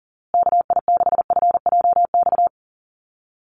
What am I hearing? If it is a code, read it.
Morse code, heard as a keyed tone, about 40 wpm, 700 Hz. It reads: KI6FJX
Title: USA Call Signs Repeated 3x 40wpm